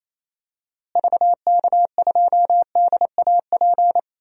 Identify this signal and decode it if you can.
Morse code, heard as a keyed tone, about 28 wpm, 700 Hz. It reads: VK2DAP
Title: DXCC-100 Call Signs - 6 Characters - Repeated 3x - 28wpm